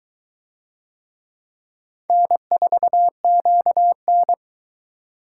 Morse code, heard as a keyed tone, about 22 wpm, 700 Hz. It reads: N4QN